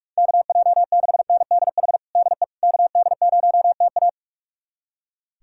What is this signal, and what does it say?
Morse code, heard as a keyed tone, about 42 wpm, 700 Hz. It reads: KJ6NDH DE KD0TA
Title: USA Callsign DE Callsign 45wpm